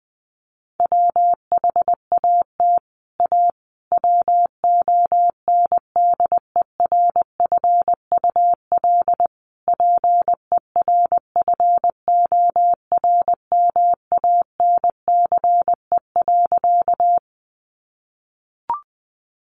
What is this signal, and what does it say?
Morse code, heard as a keyed tone, about 20 wpm, 700 Hz. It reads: WHAT A WONDERFUL PERFORMANCE.
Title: Sentences from Top 1100 Words 20wpm